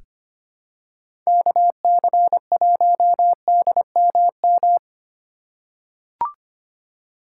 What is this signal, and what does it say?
Morse code, heard as a keyed tone, about 25 wpm, 700 Hz. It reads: KC1DMM
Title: USA Call Signs - 6 Characters - 2x Character Spacing - 25wpm